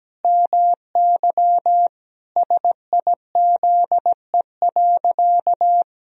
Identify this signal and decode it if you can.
Morse code, heard as a keyed tone, about 17 wpm, 700 Hz. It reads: MY SIZE.
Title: Sentences from Top 1000 Words - Review Part 2 - 17wpm